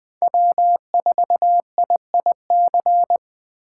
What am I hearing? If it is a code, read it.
Morse code, heard as a keyed tone, about 20 wpm, 700 Hz. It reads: W4IIC
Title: K1USN SST QSO Practice - 20wpm